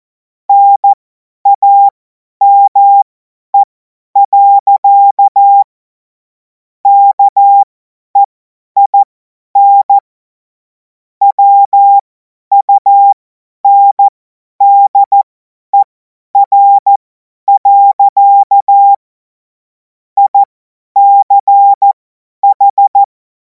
Morse code, heard as a keyed tone, about 14 wpm, spaced as for 10 wpm, 800 Hz. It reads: NAME. KEIN WUNDER. ICH
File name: Morsekurs / Witze / Witz 10. März 2026 50 F70